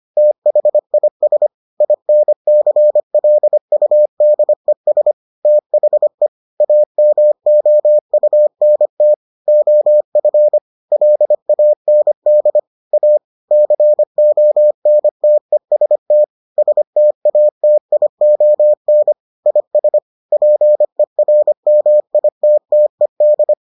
Morse code, heard as a keyed tone, about 25 wpm, 600 Hz. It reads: THIS INCLUDES THE AMOUNT OF LAND A CONTEST STATION IS PERMITTED